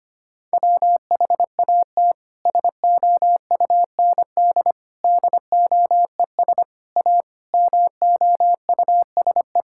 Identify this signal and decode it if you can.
Morse code, heard as a keyed tone, about 25 wpm, 700 Hz. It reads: WHAT SOUND DOES A MOUSE
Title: Sentences from Top 1000 Words - Review Part 5 - 25wpm